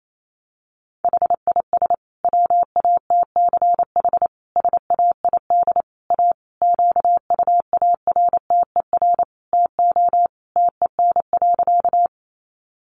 Morse code, heard as a keyed tone, about 28 wpm, 700 Hz. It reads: HIS WATCH SAID A QUARTER TO TEN.
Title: Sentences from Top 1300 Words 28wpm